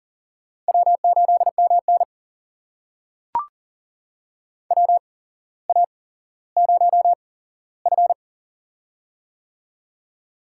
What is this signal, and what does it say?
Morse code, heard as a keyed tone, about 40 wpm, spaced as for 17 wpm, 700 Hz. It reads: W8MN WA0F
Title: USA Call Signs - 4 Characters - 8x Character Spacing - 40wpm